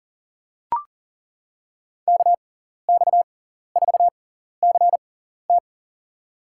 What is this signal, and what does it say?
Morse code, heard as a keyed tone, about 40 wpm, spaced as for 14 wpm, 700 Hz. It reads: KX4CT
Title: USA Call Signs - 5 Characters - 6x Character Spacing - 40wpm